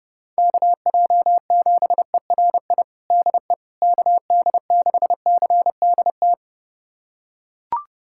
Morse code, heard as a keyed tone, about 30 wpm, 700 Hz. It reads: KJ7ERI DE KD6CDT
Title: USA Callsign DE Callsign 30wpm